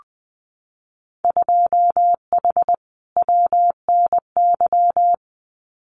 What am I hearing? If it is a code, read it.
Morse code, heard as a keyed tone, about 20 wpm, 700 Hz. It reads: 2H WNY
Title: Winter Field Day Exchanges Encore 20wpm